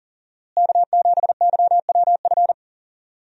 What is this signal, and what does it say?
Morse code, heard as a keyed tone, about 40 wpm, 700 Hz. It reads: K7YWF
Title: USA Call Signs - 5 Characters - Repeated 2x - 40wpm